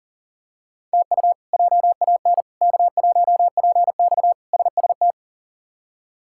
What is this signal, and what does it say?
Morse code, heard as a keyed tone, about 38 wpm, 700 Hz. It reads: TU JAN K1PX SST
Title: K1USN SST QSO Practice - 40wpm